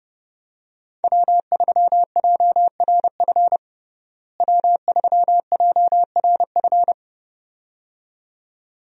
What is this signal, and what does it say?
Morse code, heard as a keyed tone, about 30 wpm, 700 Hz. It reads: W3JRF W3JRF
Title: USA Call Signs - 5 Characters - Repeated 2x - 30wpm